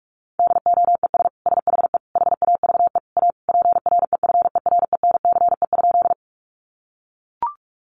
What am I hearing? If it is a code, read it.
Morse code, heard as a keyed tone, about 44 wpm, 700 Hz. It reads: DOES SHE HAVE A PREFERENCE?
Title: Sentences from Top 1500 Words 45wpm